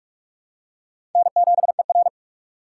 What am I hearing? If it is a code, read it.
Morse code, heard as a keyed tone, about 46 wpm, 700 Hz. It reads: N7ER
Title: USA Call Signs - 6x Character Spacing - 45wpm